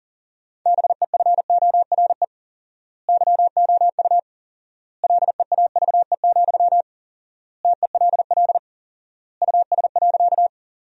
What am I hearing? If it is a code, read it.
Morse code, heard as a keyed tone, about 37 wpm, 700 Hz. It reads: BEFORE YOU LEAVE, TELL US.
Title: 4x Word-Spacing - Sentences from Top 400 Words - 40wpm